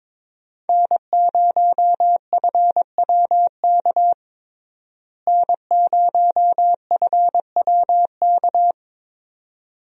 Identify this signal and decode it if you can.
Morse code, heard as a keyed tone, about 22 wpm, 700 Hz. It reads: N0FWK N0FWK
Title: USA Call Signs - 5 Characters - Repeated 2x - 22wpm